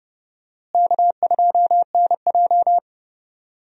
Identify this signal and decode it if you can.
Morse code, heard as a keyed tone, about 30 wpm, 700 Hz. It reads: K2NJ
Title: USA Call Signs - 4 Characters - Repeated 2x - 30wpm